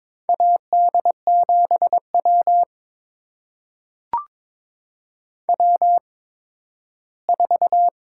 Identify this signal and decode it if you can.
Morse code, heard as a keyed tone, about 22 wpm, spaced as for 13 wpm, 700 Hz. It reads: AD7W W4
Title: USA Call Signs - 4 Characters - 8x Character Spacing - 22wpm